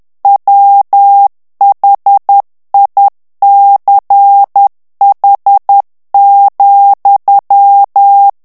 Morse code, heard as a keyed tone, about 10 wpm, 800 Hz. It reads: WHICH,
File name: ANGLAIS texte clair 3 vitesse 10 mots minute